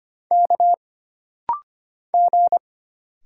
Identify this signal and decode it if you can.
Morse code, heard as a keyed tone, about 25 wpm, 700 Hz. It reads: K G
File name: Single Character (Letter:Number:Punctuation:Prosign) - Rapid-Fire - 25wpm